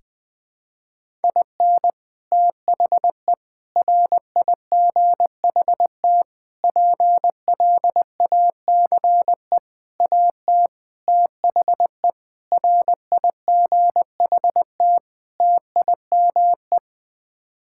Morse code, heard as a keyed tone, about 20 wpm, 700 Hz. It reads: IN THE RIGHT PLACE AT THE RIGHT TIME
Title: Sets of 8 Words - 20wpm